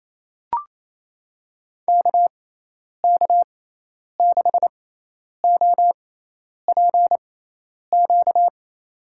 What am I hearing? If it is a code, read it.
Morse code, heard as a keyed tone, about 27 wpm, spaced as for 10 wpm, 700 Hz. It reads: KK6OPQ
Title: USA Call Signs - 6 Characters - 6x Character Spacing - 28wpm